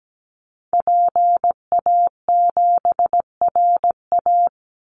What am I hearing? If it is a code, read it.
Morse code, heard as a keyed tone, about 17 wpm, 700 Hz. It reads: PA7RA
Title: Call Signs Repeated 3x - 17wpm